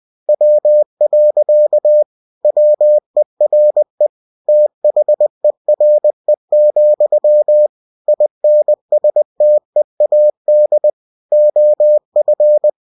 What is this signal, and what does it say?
Morse code, heard as a keyed tone, about 20 wpm, 600 Hz. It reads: W. WERE THERE, INSTEAD OF
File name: cw-books / winnie-the-pooh / winnie-20wpm-page-0023